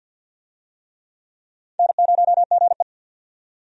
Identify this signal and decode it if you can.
Morse code, heard as a keyed tone, about 51 wpm, 700 Hz. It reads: N0GE
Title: USA Call Signs - 4 Characters - 50wpm